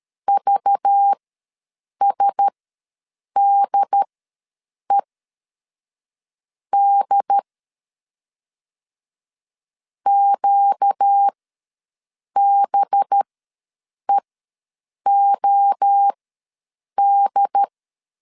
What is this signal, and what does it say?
Morse code, heard as a keyed tone, about 13 wpm, spaced as for 6 wpm, 800 Hz. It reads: VSDED QBEOD
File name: LEZIONI / 5 LBVC / 5 LBVC ws 12wpm 3159